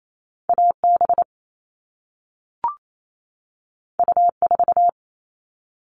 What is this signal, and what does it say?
Morse code, heard as a keyed tone, about 28 wpm, 700 Hz. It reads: AB U4
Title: All Cut Numbers - 2-Digit 28wpm